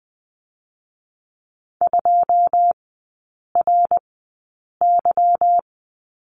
Morse code, heard as a keyed tone, about 20 wpm, spaced as for 8 wpm, 700 Hz. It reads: 2RY
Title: 152 Lesson 25e - Letter p - Three Characters 20wpm